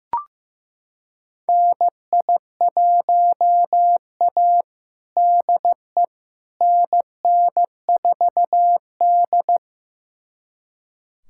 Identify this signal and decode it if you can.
Morse code, heard as a keyed tone, about 15 wpm, 700 Hz. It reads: NI1A DE NN4D
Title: USA Callsign DE Callsign - 4 Characters - 15wpm